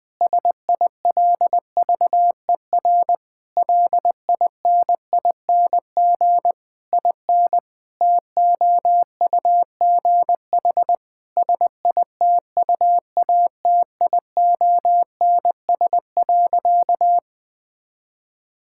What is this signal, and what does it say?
Morse code, heard as a keyed tone, about 20 wpm, 700 Hz. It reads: SILVER LINING IN TOUGH SITUATIONS.
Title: Difficult Sentences with Hints - 20wpm